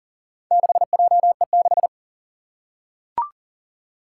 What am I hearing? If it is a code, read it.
Morse code, heard as a keyed tone, about 40 wpm, 690 Hz. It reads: 6JE6
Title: Four Challenging Letter-Number Combinations 40wpm